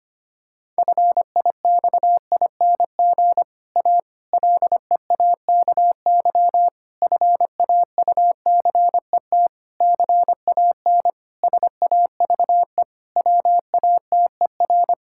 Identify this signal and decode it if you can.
Morse code, heard as a keyed tone, about 25 wpm, 700 Hz. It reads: FIXING A LEAKY FAUCET CAN SAVE WATER
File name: practice / 25wpm / Difficult Sentences with Hints - 25wpm